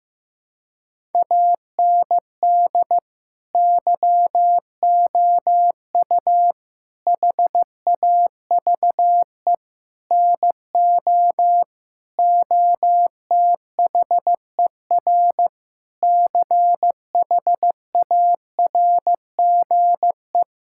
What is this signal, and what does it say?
Morse code, heard as a keyed tone, about 15 wpm, 700 Hz. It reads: AND YOU HAVE NO OTHER CHARGE
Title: Sentences from Top 1000 Words - Review Part 6 - 15wpm